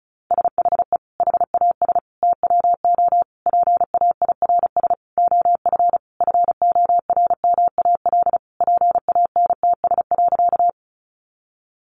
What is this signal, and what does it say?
Morse code, heard as a keyed tone, about 34 wpm, 700 Hz. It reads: SHE HAS TWO PAIRS OF FORMAL PANTS.
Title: Additional Easy Phrases 35wpm